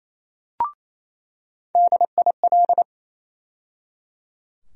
Morse code, heard as a keyed tone, about 28 wpm, 700 Hz. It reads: DIL
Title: USA Call Sign Postfixes - 3 Letters 28wpm